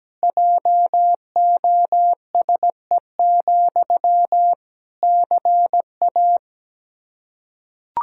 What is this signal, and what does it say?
Morse code, heard as a keyed tone, about 17 wpm, 700 Hz. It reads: JOSE, CA
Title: Practice QSO - Exchange 1 - Beginning - 17wpm